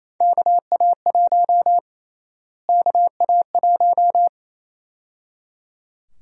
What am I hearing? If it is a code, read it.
Morse code, heard as a keyed tone, about 28 wpm, 700 Hz. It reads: KA1 KA1
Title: USA Call Sign Prefixes - Repeated 2x - 28wpm